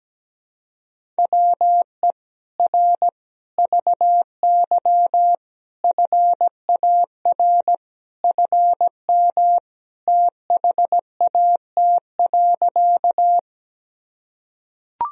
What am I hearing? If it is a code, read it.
Morse code, heard as a keyed tone, about 17 wpm, 700 Hz. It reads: WE R VY FAR FM THAT.
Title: Sentences from Top 300 Words and Abbreviations 17wpm